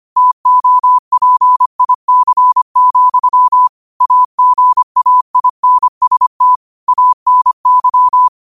Morse code, heard as a keyed tone, about 25 wpm, 1000 Hz. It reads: TOPIC, AGAINST ANY
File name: qotdfiles / 25wpm